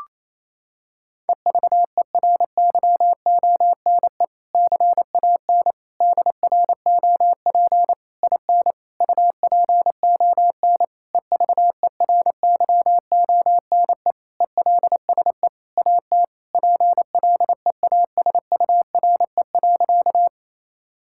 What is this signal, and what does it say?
Morse code, heard as a keyed tone, about 28 wpm, 700 Hz. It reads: EVERYONE CAN DROP IN UPON EVERYONE ELSE AT PLEASURE.
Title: Sentences from Top 1600 Words 28wpm